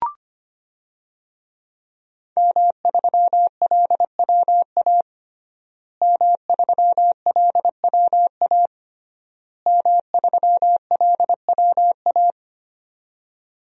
Morse code, heard as a keyed tone, about 25 wpm, 700 Hz. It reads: M3LWA M3LWA M3LWA
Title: DXCC-100 Call Signs - 5 Characters - Repeated 3x - 25wpm